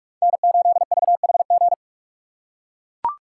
Morse code, heard as a keyed tone, about 44 wpm, 700 Hz. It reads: N8VHG